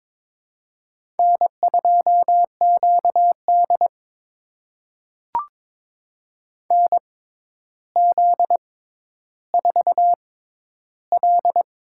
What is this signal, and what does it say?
Morse code, heard as a keyed tone, about 22 wpm, spaced as for 11 wpm, 700 Hz. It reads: N2QD NZ4L
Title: USA Call Signs - 4 Characters - 6x Character Spacing - 22wpm